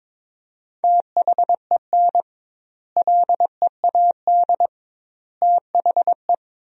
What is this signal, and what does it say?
Morse code, heard as a keyed tone, about 22 wpm, 700 Hz. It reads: THEN LEAD THE
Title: 2x Word-Spacing - Sentences from Top 500 Words - 22wpm